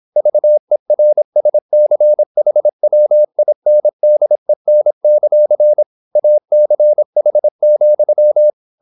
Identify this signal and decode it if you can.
Morse code, heard as a keyed tone, about 26 wpm, 600 Hz. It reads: VERSCHWINDEN; ACH,